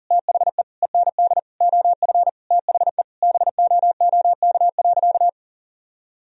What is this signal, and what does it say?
Morse code, heard as a keyed tone, about 40 wpm, 700 Hz. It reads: THE END OF THE BOOK.
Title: Sentences from Top 1000 Words - Review Part 1 - 40wpm